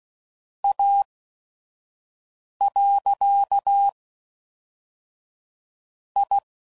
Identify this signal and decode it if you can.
Morse code, heard as a keyed tone, about 16 wpm, 800 Hz. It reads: A . I